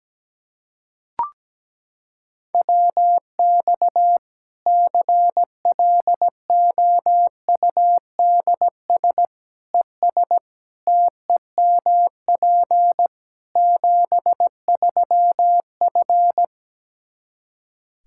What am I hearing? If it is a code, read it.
Morse code, heard as a keyed tone, about 17 wpm, 700 Hz. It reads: WX CLOUDS ES TEMP 73F
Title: Practice QSO - Exchange 2 - Info - 17wpm